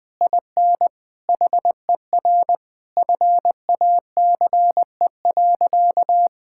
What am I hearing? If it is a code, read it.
Morse code, heard as a keyed tone, about 21 wpm, 700 Hz. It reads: IN HER FACE.
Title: Sentences from Top 1600 Words 20wpm